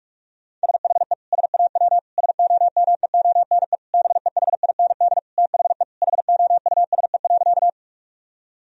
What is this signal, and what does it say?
Morse code, heard as a keyed tone, about 46 wpm, 700 Hz. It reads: SHE SAW SOMEONE BEHIND THE HOUSE.
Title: Sentences from Top 1000 Words - Review Part 4 - 45wpm